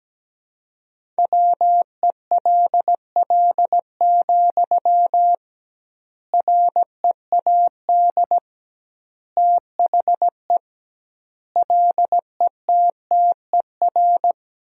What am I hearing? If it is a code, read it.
Morse code, heard as a keyed tone, about 17 wpm, 700 Hz. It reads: WELL, READ THE LETTER
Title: 2x Word-Spacing - Sentences from Top 300 Words - 17wpm